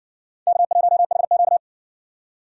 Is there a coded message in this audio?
Yes: Morse code from 0.5 to 1.6 s, heard as a keyed tone, about 60 wpm, 700 Hz. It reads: K1HQ